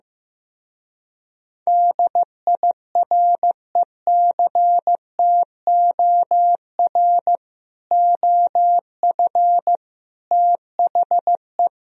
Morse code, heard as a keyed tone, about 15 wpm, 700 Hz. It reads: DIRECTOR OF THE
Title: Speed-Racing - Sets of 3 Words 15wpm